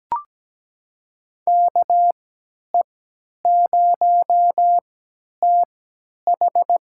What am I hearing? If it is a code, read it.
Morse code, heard as a keyed tone, about 17 wpm, spaced as for 10 wpm, 700 Hz. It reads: KE0TH
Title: USA Call Signs - 3x Character Spacing - 17wpm